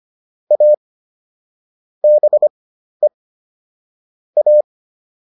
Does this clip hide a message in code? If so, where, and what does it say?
Yes: Morse code from 0.5 to 4.6 s, heard as a keyed tone, about 25 wpm, spaced as for 12 wpm, 600 Hz. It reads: A BE A